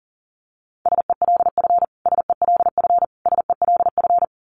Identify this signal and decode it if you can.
Morse code, heard as a keyed tone, about 40 wpm, 700 Hz. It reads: SELF SELF SELF
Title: Top 500 Words Repeated 40wpm